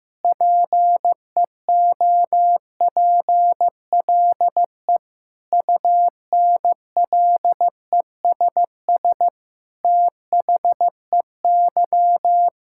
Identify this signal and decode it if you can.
Morse code, heard as a keyed tone, about 15 wpm, 700 Hz. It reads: PEOPLE UNLESS THEY
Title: Sentences from Top 1000 Words - Review Part 6 - 15wpm